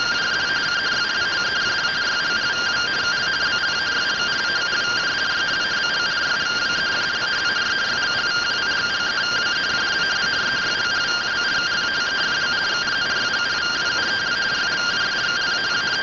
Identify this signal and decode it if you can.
Two-tone teleprinter signal, about 170 Hz shift.
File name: findings / CT SAGITAL 27.gen-04 rtty rx